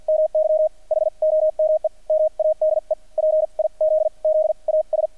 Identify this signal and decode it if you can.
Morse code, heard as a keyed tone, about 50 wpm, spaced as for 40 wpm, 620 Hz. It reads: MY HOME MADE PIZZAS
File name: archived / CRSnet